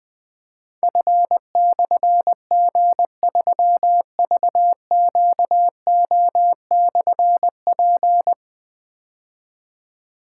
Morse code, heard as a keyed tone, about 20 wpm, 700 Hz. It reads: F/G3VQO/P
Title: Difficult Call Signs 20wpm